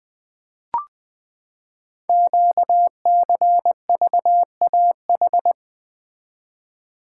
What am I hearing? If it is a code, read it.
Morse code, heard as a keyed tone, about 20 wpm, 700 Hz. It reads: QCVAH
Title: Code Groups - 5 Letters - 20wpm